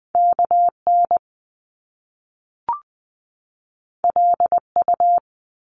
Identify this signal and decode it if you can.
Morse code, heard as a keyed tone, about 20 wpm, 700 Hz. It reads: KN LU
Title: Top Two Letter Combinations in Words 20wpm